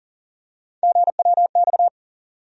Morse code, heard as a keyed tone, about 39 wpm, 700 Hz. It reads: GWX
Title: USA Call Sign Postfixes - 3 Letters 40wpm